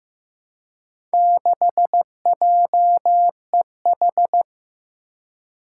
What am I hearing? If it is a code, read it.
Morse code, heard as a keyed tone, about 15 wpm, 700 Hz. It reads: 6JEH
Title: Four Challenging Letter-Number Combinations 15wpm